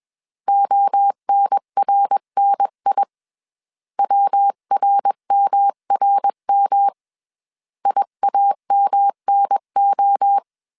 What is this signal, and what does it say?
Morse code, heard as a keyed tone, about 21 wpm, 800 Hz. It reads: ONRNI WRMRM IAMNO
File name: 20wpm